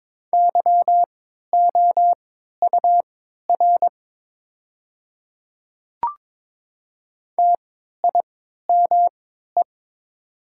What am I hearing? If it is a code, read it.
Morse code, heard as a keyed tone, about 22 wpm, spaced as for 12 wpm, 700 Hz. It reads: YOUR TIME
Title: 3x Character Spacing - Top 100 Words - 22wpm